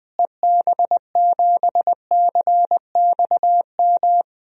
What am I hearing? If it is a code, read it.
Morse code, heard as a keyed tone, about 20 wpm, 700 Hz. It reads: EB7CXM
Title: DXCC-100 Call Signs - 6 Characters - Repeated 2x - 20wpm